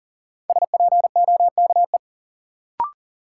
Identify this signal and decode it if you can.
Morse code, heard as a keyed tone, about 40 wpm, 700 Hz. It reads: SPOKE
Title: Sentences from Top 1200 Words - New Word Drill - 40wpm